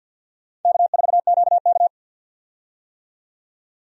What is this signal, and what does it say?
Morse code, heard as a keyed tone, about 53 wpm, 700 Hz. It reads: K4QK
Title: USA Call Signs - 4 Characters - 50wpm